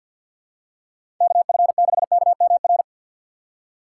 Morse code, heard as a keyed tone, about 52 wpm, 700 Hz. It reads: KF6KMR
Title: USA Call Signs - 2x Character Spacing - 50wpm